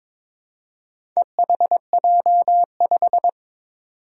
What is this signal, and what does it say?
Morse code, heard as a keyed tone, about 22 wpm, 700 Hz. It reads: EHJ5
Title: Four Challenging Letter-Number Combinations 22wpm